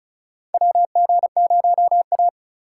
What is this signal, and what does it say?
Morse code, heard as a keyed tone, about 36 wpm, 700 Hz. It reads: WG0A
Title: USA Call Signs - 4 Characters - 2x Character Spacing - 35wpm